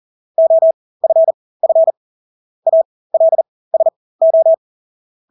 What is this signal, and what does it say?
Morse code, heard as a keyed tone, about 40 wpm, spaced as for 20 wpm, 650 Hz. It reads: OFF ALSO
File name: Engelske ord 40-20 wpm